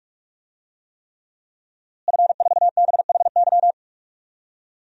Morse code, heard as a keyed tone, about 46 wpm, 700 Hz. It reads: F4BHY
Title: DXCC-100 Call Signs - 5 Characters - Repeated 3x - 45wpm